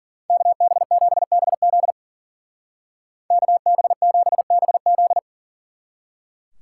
Morse code, heard as a keyed tone, about 49 wpm, 700 Hz. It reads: KB7BZ KB7BZ
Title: Speed Racing - USA 2x2 Call Signs 40wpm